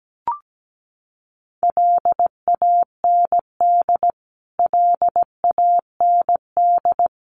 Morse code, heard as a keyed tone, about 17 wpm, 700 Hz. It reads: LAND LAND
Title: Top 200 Words Repeated 17wpm